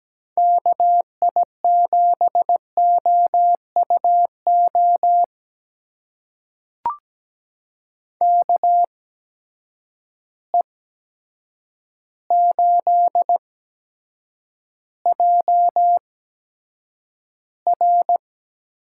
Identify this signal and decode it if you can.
Morse code, heard as a keyed tone, about 17 wpm, spaced as for 8 wpm, 700 Hz. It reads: KI7OUO KE8JR
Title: USA Call Signs - 6 Characters - 8x Character Spacing - 17wpm